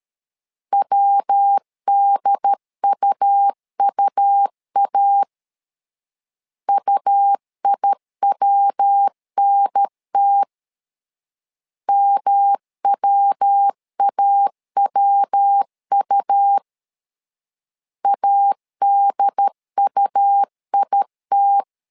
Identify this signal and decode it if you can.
Morse code, heard as a keyed tone, about 13 wpm, 800 Hz. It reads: WDUUA UIWNT MWAWU ADUIT
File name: LEZIONI / 3 MWRO / 3 MW 12wpm 1633